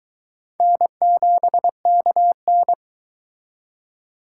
Morse code, heard as a keyed tone, about 22 wpm, 700 Hz. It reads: N7KN